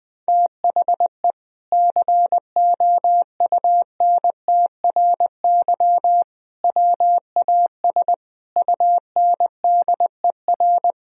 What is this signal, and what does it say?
Morse code, heard as a keyed tone, about 21 wpm, 700 Hz. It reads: THE COUNTRY WAS UNDER
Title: Sentences from Top 1000 Words - Review Part 1 - 20wpm